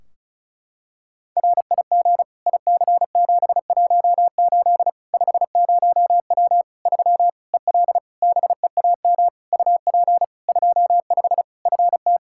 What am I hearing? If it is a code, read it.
Morse code, heard as a keyed tone, about 34 wpm, 700 Hz. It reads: RIG IC718 50W 3 EL BEAM UP 25 FT
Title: Practice QSO - Exchange 2 - Info - 35wpm